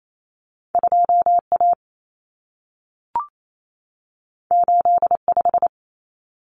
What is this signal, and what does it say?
Morse code, heard as a keyed tone, about 28 wpm, 700 Hz. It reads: 2A 85
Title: Common Cut Numbers - 2-Digit 28wpm